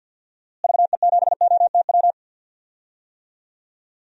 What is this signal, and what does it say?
Morse code, heard as a keyed tone, about 51 wpm, 700 Hz. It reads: VE7OTW
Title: DXCC-100 Call Signs - 6 Characters - Repeated 3x - 50wpm